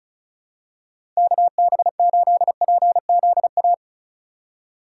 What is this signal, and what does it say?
Morse code, heard as a keyed tone, about 36 wpm, 700 Hz. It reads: KB8PZA
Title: USA Call Signs - 6 Characters - 3x Character Spacing - 35wpm